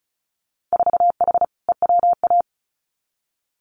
Morse code, heard as a keyed tone, about 34 wpm, 700 Hz. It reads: 4H EWA
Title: Winter Field Day Exchanges Encore 35wpm